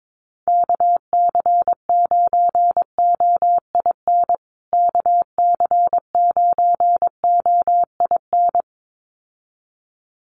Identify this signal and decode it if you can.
Morse code, heard as a keyed tone, about 22 wpm, 700 Hz. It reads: KC9OIN KC9OIN
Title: USA Call Signs Repeated 2x 22wpm